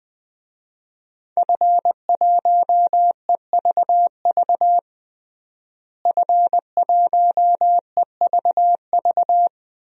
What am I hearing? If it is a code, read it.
Morse code, heard as a keyed tone, about 20 wpm, 700 Hz. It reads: F1EVV F1EVV
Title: DXCC-100 Call Signs - 5 Characters - Repeated 2x - 20wpm